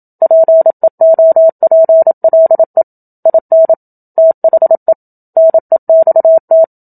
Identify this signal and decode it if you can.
Morse code, heard as a keyed tone, about 27 wpm, 650 Hz. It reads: PEOPLE IN THE NEXT
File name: Book Two / 01-Chapter 1 - 27 WPM